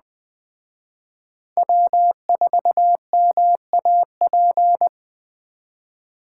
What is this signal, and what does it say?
Morse code, heard as a keyed tone, about 21 wpm, 700 Hz. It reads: W4MAP
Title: Speed Racing - USA 1x3 Call Signs 17wpm